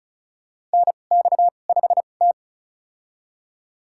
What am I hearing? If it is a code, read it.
Morse code, heard as a keyed tone, about 36 wpm, spaced as for 25 wpm, 700 Hz. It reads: NX5T
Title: USA Call Signs - 4 Characters - 2x Character Spacing - 35wpm